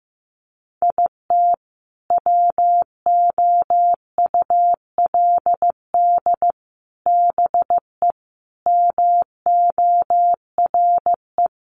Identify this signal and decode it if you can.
Morse code, heard as a keyed tone, about 15 wpm, 700 Hz. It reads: IT WOULD BE MORE
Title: Sentences from Top 1000 Words 15wpm